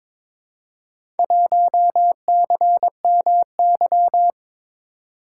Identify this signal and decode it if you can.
Morse code, heard as a keyed tone, about 22 wpm, 700 Hz. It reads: 1CMY
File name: practice / 22wpm / Four Letter-Number Combinations - 22wpm